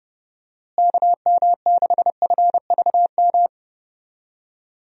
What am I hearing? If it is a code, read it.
Morse code, heard as a keyed tone, about 30 wpm, 700 Hz. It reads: KM6FVM
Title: USA Call Signs - 6 Characters - Repeated 2x - 30wpm